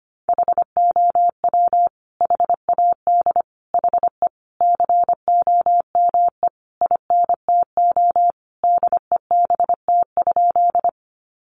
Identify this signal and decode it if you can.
Morse code, heard as a keyed tone, about 25 wpm, 700 Hz. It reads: HOW HAD HE COME INTO DEBT?
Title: Sentences from Top 1300 Words 25wpm